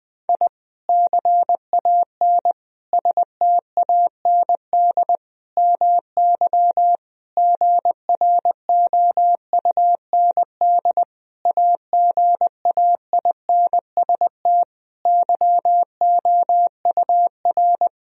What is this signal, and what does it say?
Morse code, heard as a keyed tone, about 20 wpm, 700 Hz. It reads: I CAN STAND MY GROUND AGAINST YOUR
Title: Sentences from Top 1000 Words - Review Part 2 - 20wpm